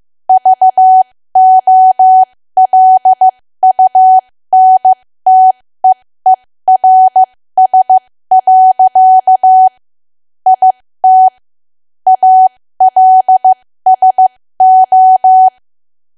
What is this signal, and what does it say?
Morse code, heard as a keyed tone, about 15 wpm, spaced as for 13 wpm, 750 Hz. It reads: VOLUNTEERS. IT ALSO
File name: cwa / W1AW-Level2 / 13WPM